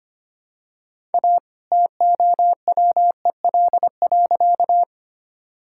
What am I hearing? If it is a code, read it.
Morse code, heard as a keyed tone, about 25 wpm, 700 Hz. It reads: A TOWEL.
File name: practice / 25wpm / All-Age Appropriate Jokes - 25wpm